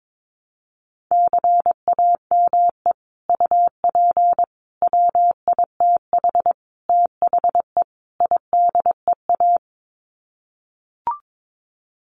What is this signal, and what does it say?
Morse code, heard as a keyed tone, about 22 wpm, 700 Hz. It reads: CAME UP WITH THE IDEA
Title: Sets of 5 Words Encore2 22wpm